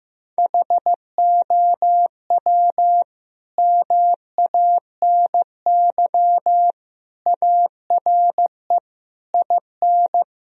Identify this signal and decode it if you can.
Morse code, heard as a keyed tone, about 14 wpm, 700 Hz. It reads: HOW MANY ARE IN
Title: Sentences from Top 1000 Words - Review Part 2 - 15wpm